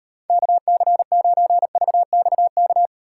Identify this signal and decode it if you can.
Morse code, heard as a keyed tone, about 36 wpm, 700 Hz. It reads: KC9VXK